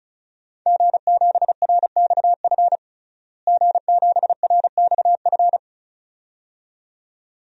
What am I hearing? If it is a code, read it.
Morse code, heard as a keyed tone, about 36 wpm, 700 Hz. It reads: G7RXF G7RXF
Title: DXCC-100 Call Signs - 5 Characters - Repeated 2x - 35wpm